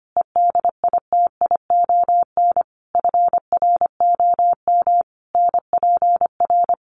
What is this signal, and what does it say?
Morse code, heard as a keyed tone, about 25 wpm, 700 Hz. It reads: EDITION FROM NPR
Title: Sets of 5 Words 25wpm